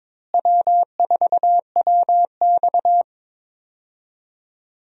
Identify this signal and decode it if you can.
Morse code, heard as a keyed tone, about 22 wpm, 700 Hz. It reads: W4WX
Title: USA Call Signs - 4 Characters - Repeated 3x - 22wpm